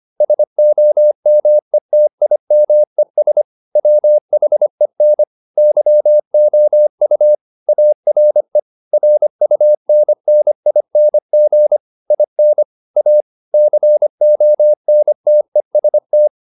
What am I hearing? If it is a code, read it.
Morse code, heard as a keyed tone, about 25 wpm, 600 Hz. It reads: SOMETIMES WHEN YOU ARE RUNNING IN A CONTEST